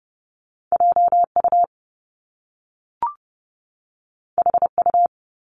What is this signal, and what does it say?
Morse code, heard as a keyed tone, about 30 wpm, 700 Hz. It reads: J U H U
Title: Top Two Letter Combinations in Words 30wpm